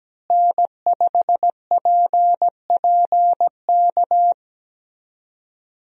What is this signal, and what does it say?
Morse code, heard as a keyed tone, about 17 wpm, 700 Hz. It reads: N5PPK
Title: USA Call Signs - 5 Characters - Repeated 3x - 17wpm